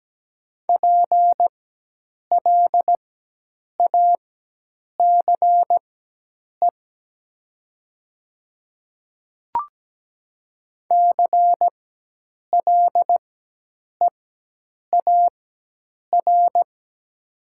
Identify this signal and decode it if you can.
Morse code, heard as a keyed tone, about 17 wpm, spaced as for 8 wpm, 700 Hz. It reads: PLACE CLEAR
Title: 4x Character Spacing - Top 500 Words - 17wpm